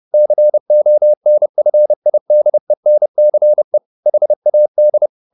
Morse code, heard as a keyed tone, about 30 wpm, 600 Hz. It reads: CONFIDENCE HAD
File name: the old man and the sea / the old man and the sea 1 30wpm